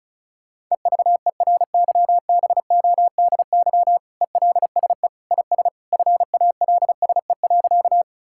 Morse code, heard as a keyed tone, about 34 wpm, 700 Hz. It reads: EVERYBODY ELSE IS FALSE.
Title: Sentences from Top 1600 Words 35wpm